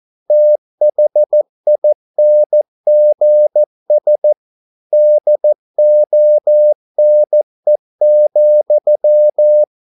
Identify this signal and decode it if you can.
Morse code, heard as a keyed tone, about 14 wpm, 600 Hz. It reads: THINGS DONE,